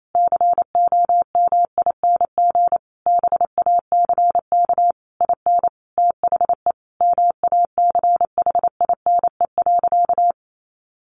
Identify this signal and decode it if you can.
Morse code, heard as a keyed tone, about 28 wpm, 700 Hz. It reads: COMING BACK IN THE MACHINE.
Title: A Sound Of Thunder 28wpm